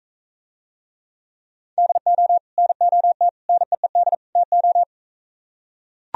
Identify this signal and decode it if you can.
Morse code, heard as a keyed tone, about 43 wpm, 700 Hz. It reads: DO NOT NEED TO